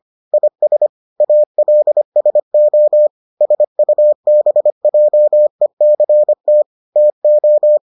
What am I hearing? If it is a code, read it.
Morse code, heard as a keyed tone, about 25 wpm, 600 Hz. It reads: IS ALSO SUBJECT TO